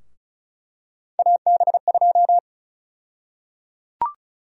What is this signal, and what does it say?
Morse code, heard as a keyed tone, about 36 wpm, 700 Hz. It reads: AB2
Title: USA Call Sign Prefixes - 35wpm